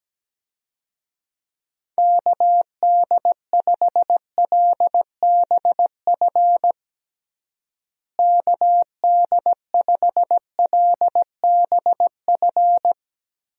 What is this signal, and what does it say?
Morse code, heard as a keyed tone, about 17 wpm, 700 Hz. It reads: KD5LBF KD5LBF
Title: USA Call Signs - 6 Characters - Repeated 3x - 17wpm